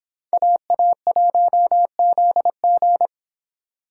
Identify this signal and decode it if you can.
Morse code, heard as a keyed tone, about 30 wpm, 700 Hz. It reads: AA1ZG